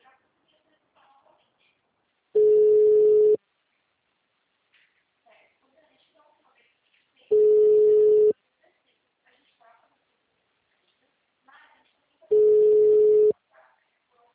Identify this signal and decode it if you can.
Ringback tone (Europe, 425 Hz), with speech.